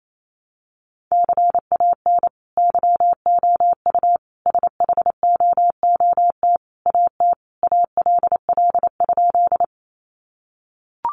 Morse code, heard as a keyed tone, about 28 wpm, 700 Hz. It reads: CAN YOU SHOOT AT ALL?
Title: Sentences from Top 1200 Words 28wpm